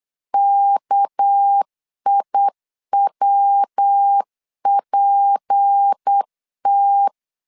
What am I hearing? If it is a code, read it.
Morse code, heard as a keyed tone, about 8 wpm, 800 Hz. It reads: KIWPT
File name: CW / DET / 1 6 1500C S B 8wpm 5142